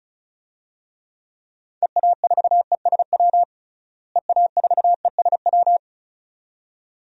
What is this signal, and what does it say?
Morse code, heard as a keyed tone, about 34 wpm, 700 Hz. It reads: EA4ESW EA4ESW
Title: DXCC-100 Call Signs - 6 Characters - Repeated 2x - 35wpm